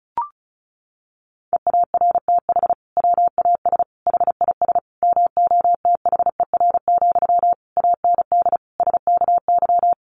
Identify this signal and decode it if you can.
Morse code, heard as a keyed tone, about 34 wpm, 700 Hz. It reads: EARTH WAS HIS MOTHER, AND SKY
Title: Sentences from Top 1200 Words 35wpm